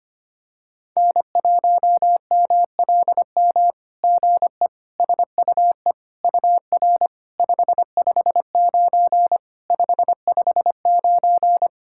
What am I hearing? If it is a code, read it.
Morse code, heard as a keyed tone, about 25 wpm, 700 Hz. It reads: N1MLM GE SUE UR 559 559
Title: SOTA QSO Practice 25wpm